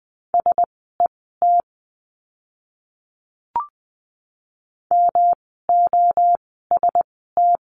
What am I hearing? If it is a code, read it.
Morse code, heard as a keyed tone, about 20 wpm, spaced as for 14 wpm, 700 Hz. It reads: SET MOST
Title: 108 Lesson 18f - Letter m - Words 20wpm